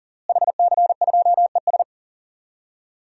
Morse code, heard as a keyed tone, about 37 wpm, 700 Hz. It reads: HC2ES
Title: DXCC-100 Call Signs - 5 Characters - Repeated 3x - 40wpm